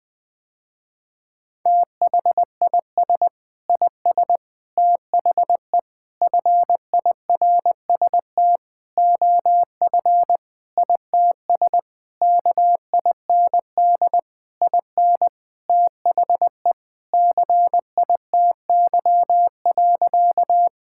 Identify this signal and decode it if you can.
Morse code, heard as a keyed tone, about 21 wpm, 700 Hz. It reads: THIS IS THE FIRST OF ITS KIND IN THE CITY.
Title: Sentences with 6-Word Phrases - Spoken Beginning - 20wpm